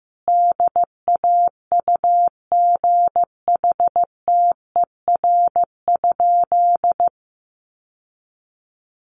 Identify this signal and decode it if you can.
Morse code, heard as a keyed tone, about 15 wpm, 700 Hz. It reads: DAUGHTER?
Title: Sentences from Top 1500 Words 15wpm